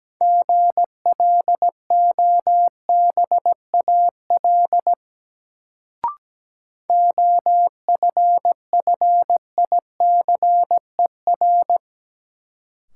Morse code, heard as a keyed tone, about 17 wpm, 700 Hz. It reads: GLOBAL OFFICER
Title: Sentences from Top 1200 Words - New Word Drill - 17wpm